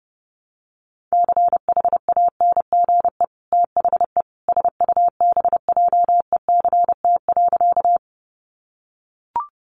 Morse code, heard as a keyed tone, about 30 wpm, 700 Hz. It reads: CHANGE THE SUBJECT.
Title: Sentences from Top 800 Words 30wpm